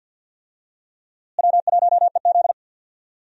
Morse code, heard as a keyed tone, about 47 wpm, 700 Hz. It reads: W1EZ
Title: USA Call Signs - 4 Characters - Repeated 2x - 50wpm